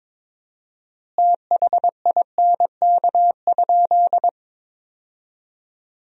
Morse code, heard as a keyed tone, about 21 wpm, 700 Hz. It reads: THINK?
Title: 4x Word-Spacing - Sentences from Top 200 Words - 22wpm